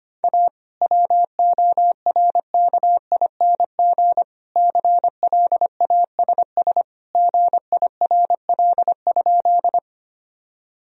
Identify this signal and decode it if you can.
Morse code, heard as a keyed tone, about 25 wpm, 700 Hz. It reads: A WORKING CLASS GIRL?
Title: Sentences from Top 1000 Words - Review Part 4 - 25wpm